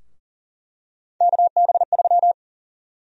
Morse code, heard as a keyed tone, about 40 wpm, 700 Hz. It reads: KB3
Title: USA Call Sign Prefixes - 40wpm